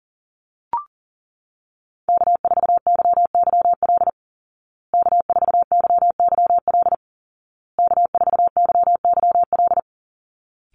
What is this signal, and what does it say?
Morse code, heard as a keyed tone, about 40 wpm, 700 Hz. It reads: K4YYL K4YYL K4YYL
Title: Call Signs Repeated 2x - 40wpm